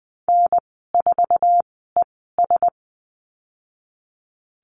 Morse code, heard as a keyed tone, about 19 wpm, spaced as for 14 wpm, 700 Hz. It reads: N4ES
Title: 048 Lesson 9g - Number 4 - Call Signs 20wpm